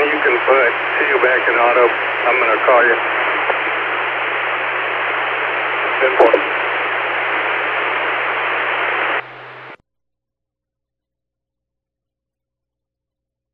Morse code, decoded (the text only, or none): none